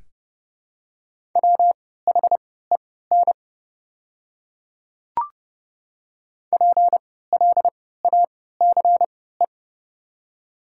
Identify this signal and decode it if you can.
Morse code, heard as a keyed tone, about 30 wpm, spaced as for 17 wpm, 700 Hz. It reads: WHEN PLACE
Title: 3x Character Spacing - Top 300 Words - 30wpm